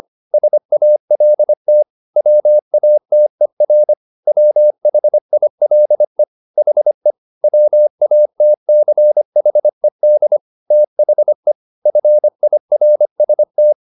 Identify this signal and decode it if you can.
Morse code, heard as a keyed tone, about 25 wpm, 600 Hz. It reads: SALT WATER WHILE HE WATCHED THE FIRST